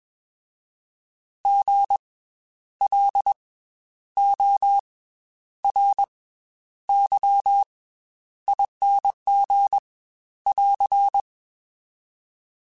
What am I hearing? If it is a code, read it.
Morse code, heard as a keyed tone, about 21 wpm, 800 Hz. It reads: G L O R Y ING <AR>
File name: Special INGs / Special INGs-mp3s / Slow ing @20